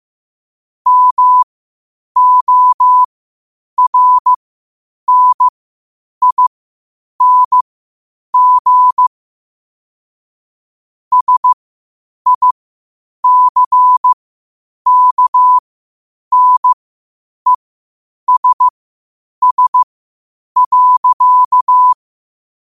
Morse code, heard as a keyed tone, about 15 wpm, spaced as for 8 wpm, 1000 Hz. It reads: MORNING SICKNESS.
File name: qotdfiles / 8wpm